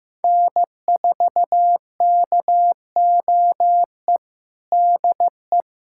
Morse code, heard as a keyed tone, about 15 wpm, 700 Hz. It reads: N4KOE DE
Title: USA Callsign DE Callsign - 5 Characters - 15wpm